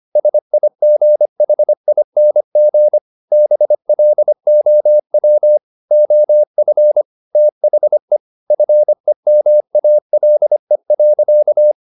Morse code, heard as a keyed tone, about 25 wpm, 600 Hz. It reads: SIGHING BLOW OF THE FEMALE.
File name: the old man and the sea part 1 25wpm